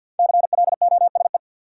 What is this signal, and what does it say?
Morse code, heard as a keyed tone, about 53 wpm, 700 Hz. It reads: CLOSE
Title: Sentences from Top 300 Words - New Word Drill - 50wpm